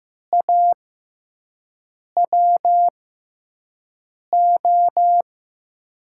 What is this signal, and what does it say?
Morse code, heard as a keyed tone, about 15 wpm, spaced as for 5 wpm, 700 Hz. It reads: AWO
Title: USA Call Signs - 6 Characters - 6x Character Spacing - 15wpm